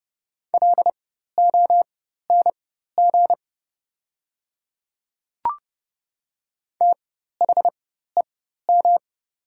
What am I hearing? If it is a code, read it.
Morse code, heard as a keyed tone, about 30 wpm, spaced as for 14 wpm, 700 Hz. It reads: LONG THEM
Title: 4x Character Spacing - Top 200 Words - 30wpm